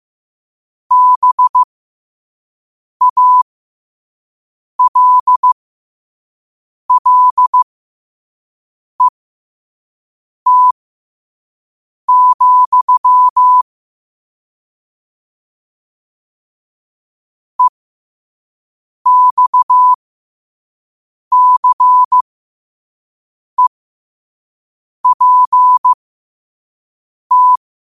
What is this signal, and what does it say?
Morse code, heard as a keyed tone, about 15 wpm, spaced as for 5 wpm, 1000 Hz. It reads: BALLET, EXCEPT
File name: qotdfiles / 5wpm